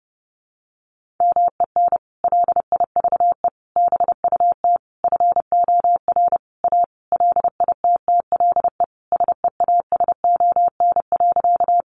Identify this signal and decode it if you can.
Morse code, heard as a keyed tone, about 30 wpm, 700 Hz. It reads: MEN LIVE BUT FOR A LITTLE SEASON.
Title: Sentences from Top 1400 Words 30wpm